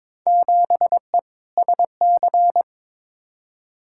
Morse code, heard as a keyed tone, about 22 wpm, 700 Hz. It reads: 7E SC
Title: Field Day Exchanges Encore - 22wpm